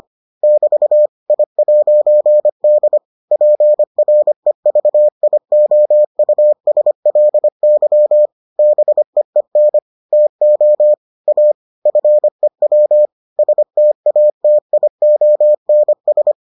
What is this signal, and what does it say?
Morse code, heard as a keyed tone, about 25 wpm, 600 Hz. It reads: <BT> I'D PREVIOUSLY BEEN TO A FEW STATIONS